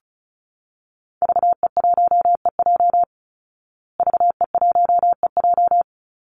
Morse code, heard as a keyed tone, about 34 wpm, 700 Hz. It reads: VE1EJ VE1EJ
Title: Call Signs Repeated 3x - 35wpm